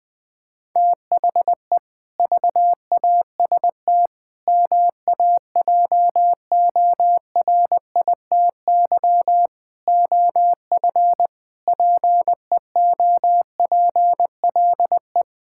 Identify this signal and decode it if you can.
Morse code, heard as a keyed tone, about 19 wpm, 700 Hz. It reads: THE VAST MAJORITY OF PEOPLE
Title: Sentences with 4-Word Phrases - 20wpm